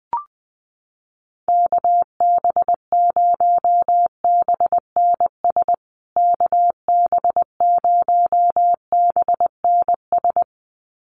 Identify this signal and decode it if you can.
Morse code, heard as a keyed tone, about 20 wpm, 700 Hz. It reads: KB0BNS KB0BNS
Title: USA Call Signs Repeated 2x 20wpm